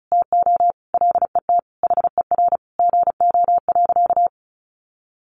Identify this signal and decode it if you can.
Morse code, heard as a keyed tone, about 34 wpm, 700 Hz. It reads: TO LET HER GO.
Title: Sentences from Top 1500 Words 35wpm